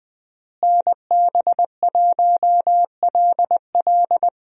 Morse code, heard as a keyed tone, about 20 wpm, 700 Hz. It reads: NB1LL
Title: USA Call Signs - 5 Characters - Repeated 3x - 20wpm